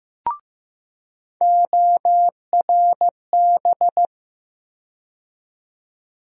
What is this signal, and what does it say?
Morse code, heard as a keyed tone, about 15 wpm, 700 Hz. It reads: ORB
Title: USA Call Sign Postfixes - 3 Letters 15wpm